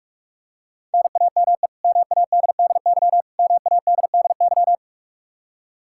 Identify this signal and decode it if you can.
Morse code, heard as a keyed tone, about 46 wpm, 700 Hz. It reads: NAME MADDY MADDY
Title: Practice QSO - Exchange 1 - Beginning - 45wpm